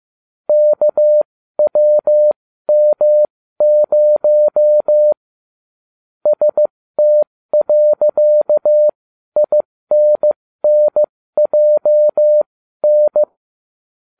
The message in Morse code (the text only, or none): KWM0 ST.INNJN